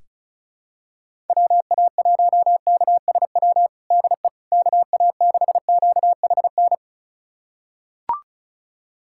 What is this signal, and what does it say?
Morse code, heard as a keyed tone, about 34 wpm, 700 Hz. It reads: WA1KSW DE KA6QHN
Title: USA Callsign DE Callsign - 6 Characters - 35wpm